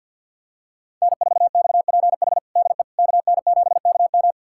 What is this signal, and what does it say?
Morse code, heard as a keyed tone, about 51 wpm, 700 Hz. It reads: N4XPH DE KN7KM
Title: USA Callsign DE Callsign - 5 Characters - 50wpm